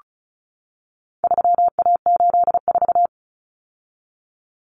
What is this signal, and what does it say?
Morse code, heard as a keyed tone, about 34 wpm, 700 Hz. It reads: 3A84
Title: Common Cut Numbers - 4-Digit 35wpm